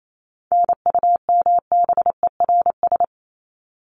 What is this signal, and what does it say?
Morse code, heard as a keyed tone, about 28 wpm, 700 Hz. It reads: NUMBERS
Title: ES:S Suffix - Top 100 Words with Base Words 28wpm